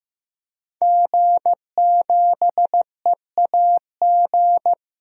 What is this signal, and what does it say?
Morse code, heard as a keyed tone, about 15 wpm, 700 Hz. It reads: G7EAG
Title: DXCC-100 Call Signs - 5 Characters - 15wpm